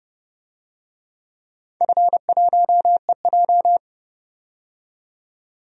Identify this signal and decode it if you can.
Morse code, heard as a keyed tone, about 30 wpm, 700 Hz. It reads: F1EJ
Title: DXCC-100 Call Signs - 4 Characters - 30wpm